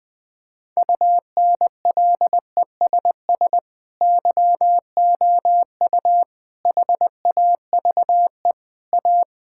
Morse code, heard as a keyed tone, about 20 wpm, 700 Hz. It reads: UNLESS YOU HAVE A
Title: Sentences from Top 1000 Words - Review Part 5 - 20wpm